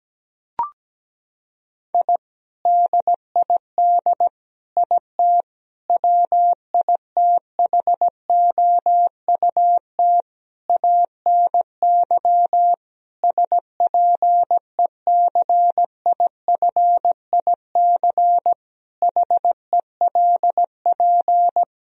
Morse code, heard as a keyed tone, about 17 wpm, 700 Hz. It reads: I DID IT WITHOUT ANY SPECIFIC HELP
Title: Sentences from Top 1000 Words - Review Part 3 - 17wpm